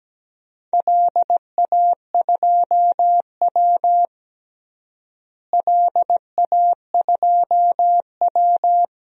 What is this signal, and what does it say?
Morse code, heard as a keyed tone, about 17 wpm, 700 Hz. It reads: LA2W LA2W
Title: DXCC-100 Call Signs - 4 Characters - Repeated 3x - 17wpm